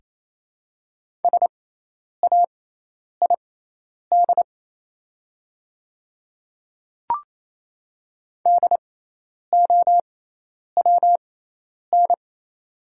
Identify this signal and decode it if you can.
Morse code, heard as a keyed tone, about 27 wpm, spaced as for 10 wpm, 700 Hz. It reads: SAID DOWN
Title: 6x Character Spacing - Top 100 Words - 28wpm